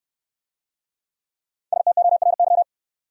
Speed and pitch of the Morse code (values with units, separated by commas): 69 wpm, 700 Hz